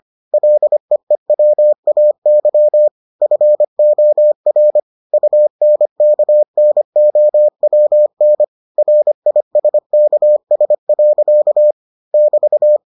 Morse code, heard as a keyed tone, about 25 wpm, 600 Hz. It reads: LEEWAY FOR UNKNOWN RISKS. <BT>